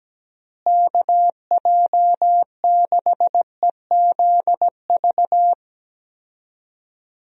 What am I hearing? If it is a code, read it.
Morse code, heard as a keyed tone, about 17 wpm, 700 Hz. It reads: KJ6EZV
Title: USA Call Signs - 6 Characters - Repeated 2x - 17wpm